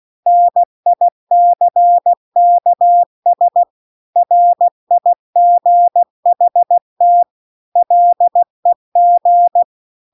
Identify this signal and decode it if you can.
Morse code, heard as a keyed tone, about 16 wpm, 700 Hz. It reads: NICKS RIGHT LEG